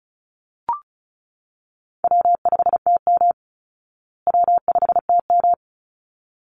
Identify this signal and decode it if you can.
Morse code, heard as a keyed tone, about 34 wpm, 700 Hz. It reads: W5TM W5TM
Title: Call Signs Repeated 3x - 35wpm